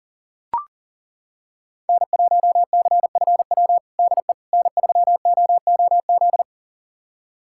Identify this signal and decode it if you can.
Morse code, heard as a keyed tone, about 40 wpm, 700 Hz. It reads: N1CFW DE N3OOZ
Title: USA Callsign DE Callsign 40wpm